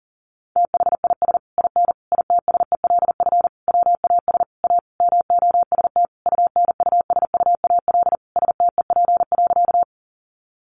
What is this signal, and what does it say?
Morse code, heard as a keyed tone, about 40 wpm, 700 Hz. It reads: THIS IN ITSELF WAS A MOST UNUSUAL STEP.
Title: Sentences from Top 1400 Words 40wpm